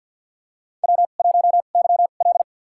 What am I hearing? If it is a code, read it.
Morse code, heard as a keyed tone, about 49 wpm, spaced as for 35 wpm, 700 Hz. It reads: W1YL
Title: USA Call Signs - 4 Characters - 2x Character Spacing - 50wpm